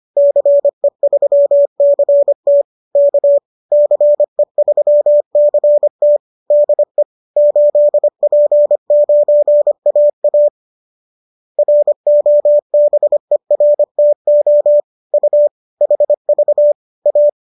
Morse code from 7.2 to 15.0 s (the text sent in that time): 8P9AA ROBERTO